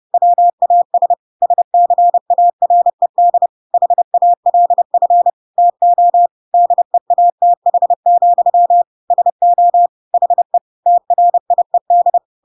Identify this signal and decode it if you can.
Morse code, heard as a keyed tone, about 30 wpm, 700 Hz. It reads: WAS SCARED HALF TO DEATH, SO HE TRIED
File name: Pinocchio / Pinocchio ch1 30 30 ik1hge